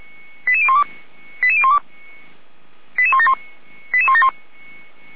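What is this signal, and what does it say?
Five-tone selective call (ZVEI), heard as quick runs of five tones, tones of about 70 ms. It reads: 88012 88012 88171 88171
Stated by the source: ZVEI five-tone selective call received by SDR.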